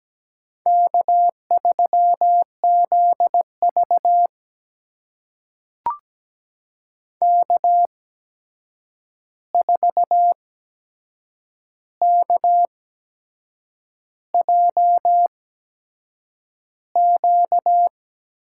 Morse code, heard as a keyed tone, about 17 wpm, spaced as for 7 wpm, 700 Hz. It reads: K3ZV K4KJQ
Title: USA Call Signs - 8x Character Spacing - 17wpm